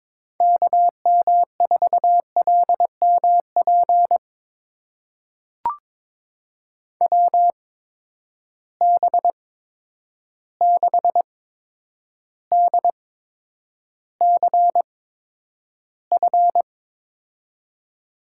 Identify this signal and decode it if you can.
Morse code, heard as a keyed tone, about 22 wpm, spaced as for 9 wpm, 700 Hz. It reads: KM4LMP WB6DCF
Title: USA Call Signs - 6 Characters - 8x Character Spacing - 22wpm